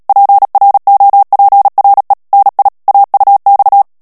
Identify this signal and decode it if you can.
Morse code, heard as a keyed tone, about 37 wpm, 800 Hz. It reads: PROPRE NI AUX
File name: CW / FICHIERS-MP3 / FRANCAIS TEXTE CLAIR 2 VITESSE 35 MOTS MINUTES